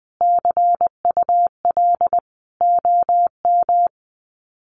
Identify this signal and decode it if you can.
Morse code, heard as a keyed tone, about 20 wpm, 700 Hz. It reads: CUL OM
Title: Common QSO Phrases 20wpm